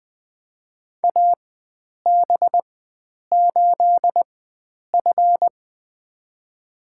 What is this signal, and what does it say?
Morse code, heard as a keyed tone, about 20 wpm, spaced as for 9 wpm, 700 Hz. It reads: AB8F
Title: USA Call Signs - 4 Characters - 4x Character Spacing - 20wpm